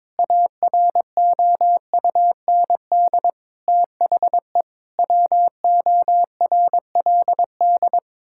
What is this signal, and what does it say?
Morse code, heard as a keyed tone, about 22 wpm, 700 Hz. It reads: AROUND THE WORLD
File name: practice / 22wpm / Sets of 7 Words - 22wpm